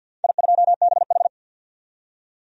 Morse code, heard as a keyed tone, about 53 wpm, 700 Hz. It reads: I2BH
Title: DXCC-100 Call Signs - 4 Characters - 50wpm